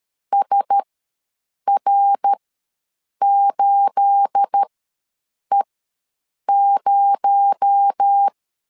Morse code, heard as a keyed tone, about 13 wpm, spaced as for 7 wpm, 800 Hz. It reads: SR8E0
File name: LEZIONI / 9 286 ws 12wpm 3351